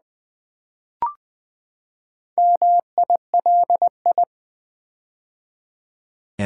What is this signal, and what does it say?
Morse code, heard as a keyed tone, about 21 wpm, 700 Hz. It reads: MILI
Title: Build A Word - 4 Letters At A Time - 8-Letter Words - 20wpm